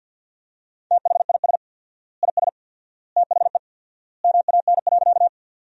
Morse code, heard as a keyed tone, about 50 wpm, 700 Hz. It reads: THIS IS THE MAN.
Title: 4x Word-Spacing - Sentences from Top 200 Words - 50wpm